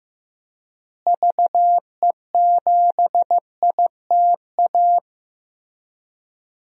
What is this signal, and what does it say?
Morse code, heard as a keyed tone, about 15 wpm, 700 Hz. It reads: VE7ITA
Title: DXCC-100 Call Signs - 6 Characters - Repeated 2x - 15wpm